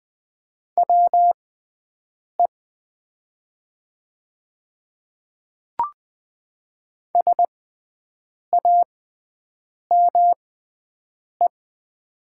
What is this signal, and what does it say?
Morse code, heard as a keyed tone, about 20 wpm, spaced as for 7 wpm, 700 Hz. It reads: WE SAME
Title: 6x Character Spacing - Top 200 Words - 20wpm